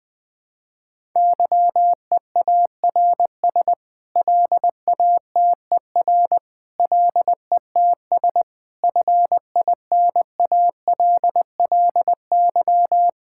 Morse code, heard as a keyed tone, about 20 wpm, 700 Hz. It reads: YEARS LATER LETS FINALLY
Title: Sets of 8 Words - 20wpm